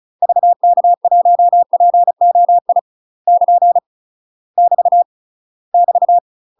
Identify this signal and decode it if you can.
Morse code, heard as a keyed tone, about 35 wpm, 700 Hz. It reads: VK1POI <KN> <BT> <BT>
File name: UK MASTER QSOs TRAINING / 35WPM / UK MASTER QSOs 0025